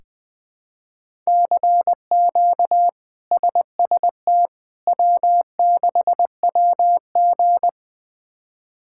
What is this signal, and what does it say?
Morse code, heard as a keyed tone, about 20 wpm, 700 Hz. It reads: CQ SST W6WG
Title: K1USN SST QSO Practice - 20wpm